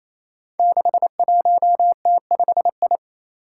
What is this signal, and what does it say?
Morse code, heard as a keyed tone, about 28 wpm, 700 Hz. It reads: 61T5I
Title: Five Challenging Letter-Number Combinations 28wpm